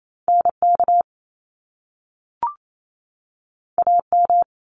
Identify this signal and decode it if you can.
Morse code, heard as a keyed tone, about 28 wpm, 700 Hz. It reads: NK AM
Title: Top Two Letter Combinations in Words 28wpm